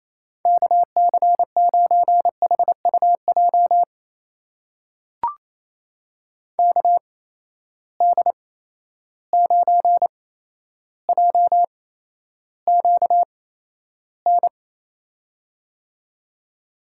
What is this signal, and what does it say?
Morse code, heard as a keyed tone, about 28 wpm, spaced as for 12 wpm, 700 Hz. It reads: KC9HUJ KD9JQN
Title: USA Call Signs - 6 Characters - 8x Character Spacing - 28wpm